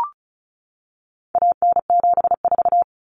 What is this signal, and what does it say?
Morse code, heard as a keyed tone, about 34 wpm, 700 Hz. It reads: AN74
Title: Common Cut Numbers - 4-Digit 35wpm